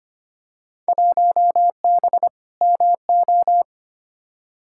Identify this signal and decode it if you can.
Morse code, heard as a keyed tone, about 25 wpm, 700 Hz. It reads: 1B MO
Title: Field Day Exchanges - 25wpm